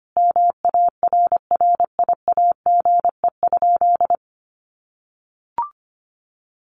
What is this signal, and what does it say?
Morse code, heard as a keyed tone, about 25 wpm, 700 Hz. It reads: MARRIAGE?
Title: Sentences from Top 1400 Words 25wpm